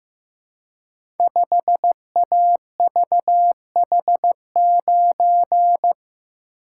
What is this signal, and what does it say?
Morse code, heard as a keyed tone, about 15 wpm, 700 Hz. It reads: 5AVH9
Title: Code Groups - 5 Letter:Numbers - 15wpm